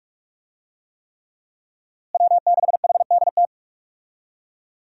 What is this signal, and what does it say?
Morse code, heard as a keyed tone, about 46 wpm, 700 Hz. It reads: W6HDT
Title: USA Call Signs - 5 Characters - Repeated 3x - 45wpm